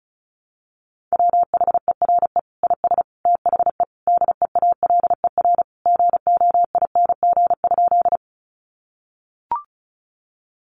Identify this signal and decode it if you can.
Morse code, heard as a keyed tone, about 34 wpm, 700 Hz. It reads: WHERE IS THE DEALER GOING?
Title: Sentences from Top 1100 Words 35wpm